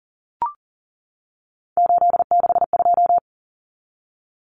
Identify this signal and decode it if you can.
Morse code, heard as a keyed tone, about 40 wpm, 700 Hz. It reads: 862
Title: Numbers - 3-Digit 40wpm